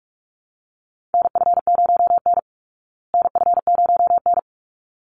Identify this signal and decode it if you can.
Morse code, heard as a keyed tone, about 44 wpm, 700 Hz. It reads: NF0N NF0N
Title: Call Signs Repeated 3x - 45wpm